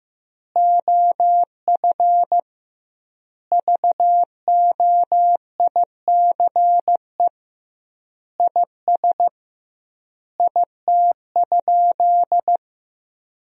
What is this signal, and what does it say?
Morse code, heard as a keyed tone, about 15 wpm, 700 Hz. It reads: OF VOICE IS IT?
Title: 2x Word-Spacing - Sentences from Top 500 Words - 15wpm